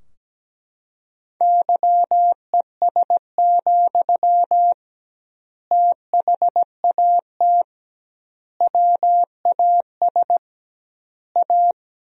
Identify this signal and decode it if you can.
Morse code, heard as a keyed tone, about 17 wpm, 700 Hz. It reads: YES, THAT WAS A
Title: 2x Word-Spacing - Sentences from Top 500 Words - 17wpm